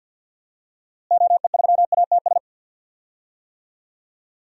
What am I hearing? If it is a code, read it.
Morse code, heard as a keyed tone, about 48 wpm, 700 Hz. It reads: OE3ATS
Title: DXCC-100 Call Signs - 6 Characters - Repeated 3x - 50wpm